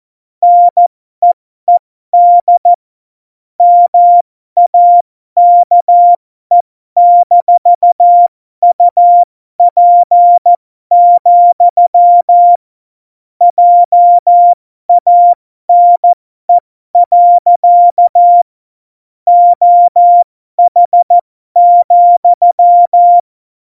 Morse code, heard as a keyed tone, about 14 wpm, spaced as for 12 wpm, 700 Hz. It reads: NEED MAKE-UP, JANE. OH,
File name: Morsekurs / Witze